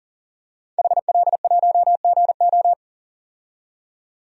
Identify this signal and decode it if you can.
Morse code, heard as a keyed tone, about 37 wpm, 700 Hz. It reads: HL1GO
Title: DXCC-100 Call Signs - 5 Characters - 40wpm